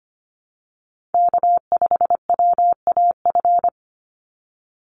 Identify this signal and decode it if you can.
Morse code, heard as a keyed tone, about 25 wpm, 700 Hz. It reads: K5WAF
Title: Call Signs Repeated 2x - 25wpm